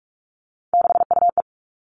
Morse code, heard as a keyed tone, about 44 wpm, 700 Hz. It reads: 6UE
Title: All Cut Numbers - 3-Digit 45wpm